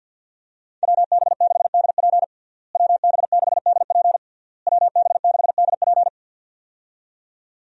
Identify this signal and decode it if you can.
Morse code, heard as a keyed tone, about 51 wpm, 700 Hz. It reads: WB6DP WB6DP WB6DP
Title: USA Call Signs - 5 Characters - Repeated 3x - 50wpm